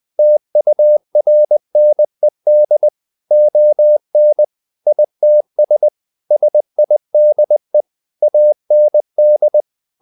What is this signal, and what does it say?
Morse code, heard as a keyed tone, about 20 wpm, 600 Hz. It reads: TURNED ON ITS SIDE AND
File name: the old man and the sea / the old man and the sea part 1 20wpm